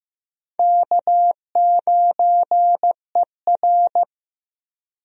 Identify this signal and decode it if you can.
Morse code, heard as a keyed tone, about 15 wpm, 700 Hz. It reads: K9ER
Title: USA Call Signs - 4 Characters - Repeated 3x - 15wpm